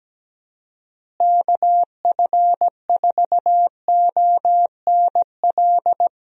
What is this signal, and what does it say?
Morse code, heard as a keyed tone, about 17 wpm, 700 Hz. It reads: KF4ONL
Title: USA Call Signs - 6 Characters - Repeated 2x - 17wpm